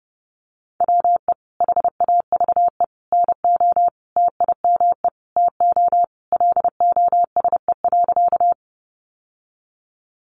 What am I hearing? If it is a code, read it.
Morse code, heard as a keyed tone, about 30 wpm, 700 Hz. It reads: WE HAVE NO TIME TO LOSE.
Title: Sentences from Top 1100 Words 30wpm